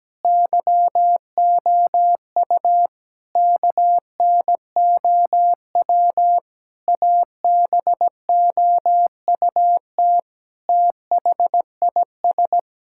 Morse code, heard as a keyed tone, about 17 wpm, 700 Hz. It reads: YOU KNOW ABOUT THIS
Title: Sentences from Top 1000 Words - Review Part 6 - 17wpm